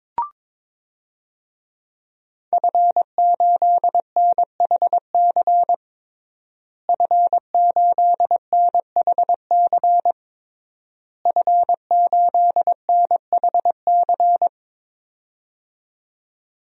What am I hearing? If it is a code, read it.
Morse code, heard as a keyed tone, about 22 wpm, 700 Hz. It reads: F8NHC F8NHC F8NHC
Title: DXCC-100 Call Signs - 5 Characters - Repeated 3x - 22wpm